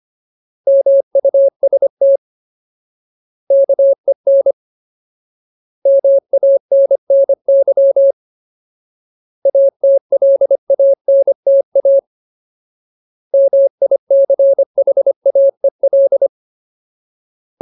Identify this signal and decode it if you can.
Morse code, heard as a keyed tone, about 25 wpm, 550 Hz. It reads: MUST KEN MANNY ATLANTA MICHAEL
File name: Ultimate Word List / 25WPM Ultimate